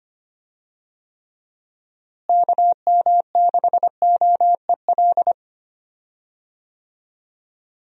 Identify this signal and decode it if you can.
Morse code, heard as a keyed tone, about 25 wpm, 700 Hz. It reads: KM6OEL
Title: USA Call Signs - 6 Characters - Repeated 3x - 25wpm